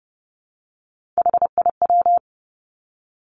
Morse code, heard as a keyed tone, about 30 wpm, 700 Hz. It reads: HIW
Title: Three Challenging Letter-Number Combinations 30wpm